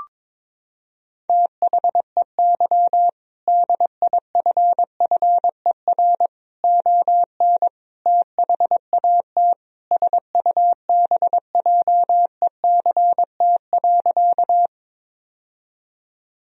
Morse code, heard as a keyed tone, about 22 wpm, 700 Hz. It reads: THEY DIFFER ON THAT SUBJECT.
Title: Sentences from Top 1000 Words - Review Part 1 - 22wpm